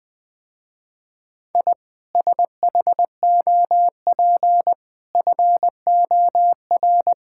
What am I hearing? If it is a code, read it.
Morse code, heard as a keyed tone, about 20 wpm, 700 Hz. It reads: I SHOP FOR
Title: Easy Sentences with Hints - 20wpm